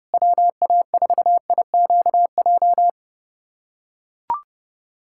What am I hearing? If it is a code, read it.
Morse code, heard as a keyed tone, about 31 wpm, 690 Hz. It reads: WA4IQJ
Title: USA Callsign DE Callsign - 6 Characters - 30wpm